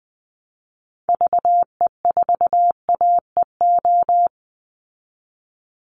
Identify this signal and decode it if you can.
Morse code, heard as a keyed tone, about 20 wpm, 700 Hz. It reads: VE4AEO
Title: Call Signs Repeated 3x - 20wpm